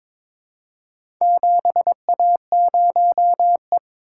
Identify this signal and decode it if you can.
Morse code, heard as a keyed tone, about 22 wpm, 700 Hz. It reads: 7A0E
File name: practice / 22wpm / Four Letter-Number Combinations - 22wpm